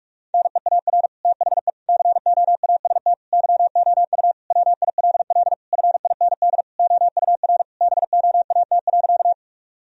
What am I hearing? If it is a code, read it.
Morse code, heard as a keyed tone, about 46 wpm, 700 Hz. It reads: NEAR THE COAST YOU WILL FIND OUR BOAT.
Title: Sentences from Top 1000 Words - Review Part 4 - 45wpm